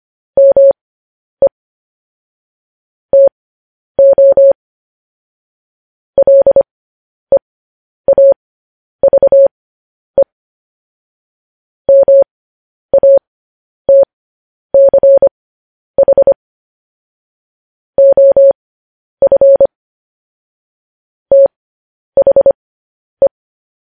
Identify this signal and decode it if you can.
Morse code, heard as a keyed tone, about 25 wpm, spaced as for 10 wpm, 550 Hz. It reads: ME TO LEAVE MATCH OF THE
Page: News Headlines in Morse Code at 10 WPM